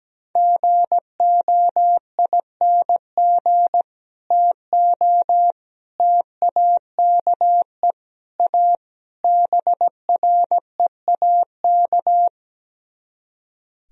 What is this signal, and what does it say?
Morse code, heard as a keyed tone, about 17 wpm, 700 Hz. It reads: GOING TO TAKE A BREAK
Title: Speed-Racing - Sets of 5 Words 17wpm